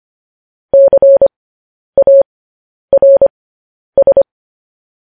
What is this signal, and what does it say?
Morse code, heard as a keyed tone, about 25 wpm, spaced as for 10 wpm, 550 Hz. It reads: CARS
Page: news headlines in morse code at 10 wpm